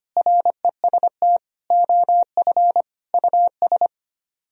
Morse code, heard as a keyed tone, about 33 wpm, 700 Hz. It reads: REST OF US